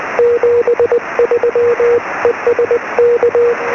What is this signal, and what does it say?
Morse code, heard as a keyed tone, about 20 wpm, 480 Hz, with radio noise. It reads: 73 ES K
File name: sm7lcb / 07-sm1fmt-1